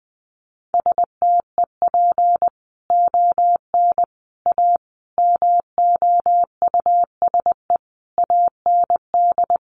Morse code, heard as a keyed tone, about 20 wpm, 700 Hz. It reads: STEP ON A MOUSE AND
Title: A Sound Of Thunder 20wpm